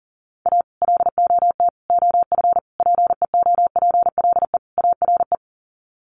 Morse code, heard as a keyed tone, about 40 wpm, 700 Hz. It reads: A LOT OF PEOPLE ARE
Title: Sets of 5 Words 40wpm